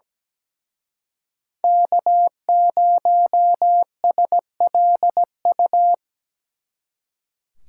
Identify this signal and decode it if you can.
Morse code, heard as a keyed tone, about 20 wpm, 700 Hz. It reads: K0SLU
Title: Speed Racing - USA Call Signs 17wpm